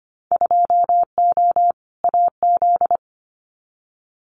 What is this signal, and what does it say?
Morse code, heard as a keyed tone, about 25 wpm, 700 Hz. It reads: 2O AZ
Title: Winter Field Day Exchanges 25wpm